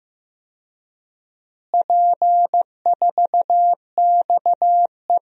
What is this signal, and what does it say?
Morse code, heard as a keyed tone, about 15 wpm, 700 Hz. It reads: P4XE
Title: DXCC-100 Call Signs - 4 Characters - Repeated 3x - 15wpm